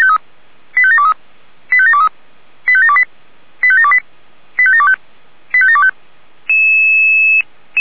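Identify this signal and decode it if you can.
Five-tone selective call (ZVEI), heard as quick runs of five tones, tones of about 70 ms. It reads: 85723 85723 85728 85728 85725 85725
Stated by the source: ZVEI five-tone call received by SDR.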